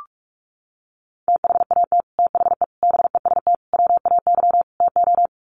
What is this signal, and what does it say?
Morse code, heard as a keyed tone, about 44 wpm, 700 Hz. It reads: THAT THE BEST WAY TO
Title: Sets of 5 Words Encore2 45wpm